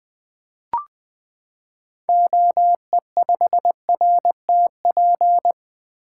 Morse code, heard as a keyed tone, about 20 wpm, 700 Hz. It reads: OE5RTP
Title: 2023 SOTA Activator Call Signs - 20wpm